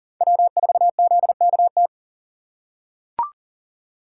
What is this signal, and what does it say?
Morse code, heard as a keyed tone, about 40 wpm, 700 Hz. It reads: W4ZKT
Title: USA Call Signs - 3x Character Spacing - 40wpm